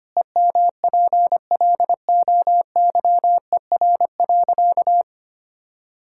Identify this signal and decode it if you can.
Morse code, heard as a keyed tone, about 25 wpm, 700 Hz. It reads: EMPLOYER.
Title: Pangram Challenge v2 - 25wpm